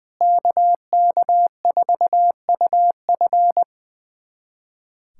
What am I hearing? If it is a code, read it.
Morse code, heard as a keyed tone, about 20 wpm, 700 Hz. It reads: KK4UF
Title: USA Callsign DE Callsign - 5 Characters - 20wpm